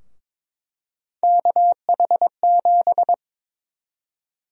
Morse code, heard as a keyed tone, about 21 wpm, 700 Hz. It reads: KH7
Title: USA Call Sign Prefixes - 22wpm